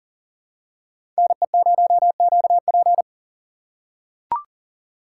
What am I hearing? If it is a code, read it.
Morse code, heard as a keyed tone, about 40 wpm, 700 Hz. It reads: NE0QP
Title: USA Call Signs - 6x Character Spacing - 40wpm